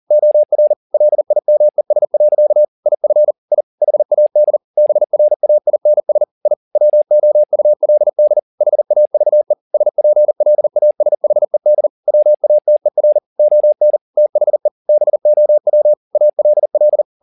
Morse code, heard as a keyed tone, about 40 wpm, 600 Hz. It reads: OR LIMES. IF I HAD BRAINS I WOULD HAVE SPLASHED WATER ON THE BOW ALL
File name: the old man and the sea 2 40wpm